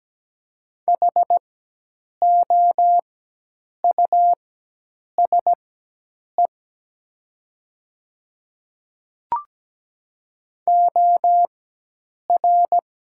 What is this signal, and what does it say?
Morse code, heard as a keyed tone, about 17 wpm, spaced as for 8 wpm, 700 Hz. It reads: HOUSE OR
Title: 4x Character Spacing - Top 300 Words - 17wpm